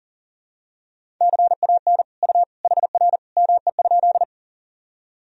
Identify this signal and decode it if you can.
Morse code, heard as a keyed tone, about 40 wpm, 700 Hz. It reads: CAN U HR ME?
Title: Sentences from Top 300 Words and Abbreviations 40wpm